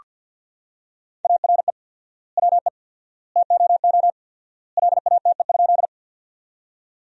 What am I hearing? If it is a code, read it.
Morse code, heard as a keyed tone, about 47 wpm, 700 Hz. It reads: ARE WE TOO LATE?
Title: 4x Word-Spacing - Sentences from Top 300 Words - 50wpm